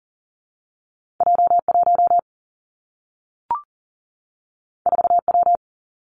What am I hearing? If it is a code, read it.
Morse code, heard as a keyed tone, about 42 wpm, 700 Hz. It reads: J 1 4 W
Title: Two Challenging Letter-Number Combinations 40wpm